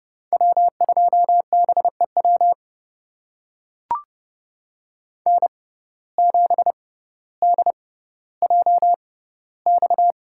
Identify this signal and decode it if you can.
Morse code, heard as a keyed tone, about 31 wpm, spaced as for 16 wpm, 690 Hz. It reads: W2BEW N7DJX
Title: USA Call Signs - 5 Characters - 6x Character Spacing - 30wpm